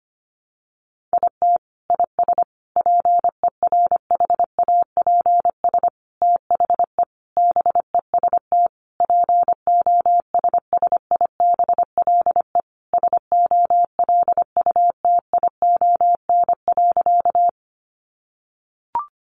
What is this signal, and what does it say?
Morse code, heard as a keyed tone, about 25 wpm, 700 Hz. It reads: IT IS PERHAPS THE BEST POSSIBLE SOLUTION.
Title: Sentences from Top 1300 Words 25wpm